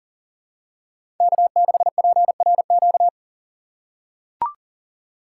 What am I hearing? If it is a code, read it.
Morse code, heard as a keyed tone, about 40 wpm, 700 Hz. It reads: K6PRQ
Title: USA Call Signs - 5 Characters - 6x Character Spacing - 40wpm